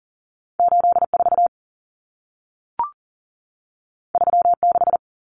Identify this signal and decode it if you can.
Morse code, heard as a keyed tone, about 40 wpm, 700 Hz. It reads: 84 36
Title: Common Cut Numbers - 2-Digit 40wpm